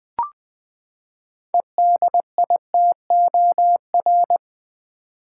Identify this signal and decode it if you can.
Morse code, heard as a keyed tone, about 20 wpm, 700 Hz. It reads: EDITOR
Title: Six Letter Words - 20wpm